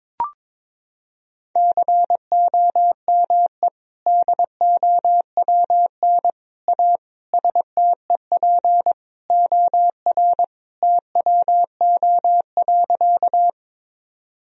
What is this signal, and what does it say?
Morse code, heard as a keyed tone, about 22 wpm, 700 Hz. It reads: COME DOWN A STEP OR TWO.
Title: Sentences from Top 1000 Words - Review Part 2 - 22wpm